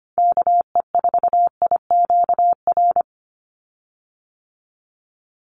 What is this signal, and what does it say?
Morse code, heard as a keyed tone, about 25 wpm, 700 Hz. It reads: KE4IQR
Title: USA Call Signs 25wpm